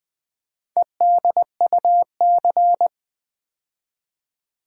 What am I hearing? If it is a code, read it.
Morse code, heard as a keyed tone, about 20 wpm, 700 Hz. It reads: EDUC
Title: Build A Word - 4 Letters At A Time - 8-Letter Words - 20wpm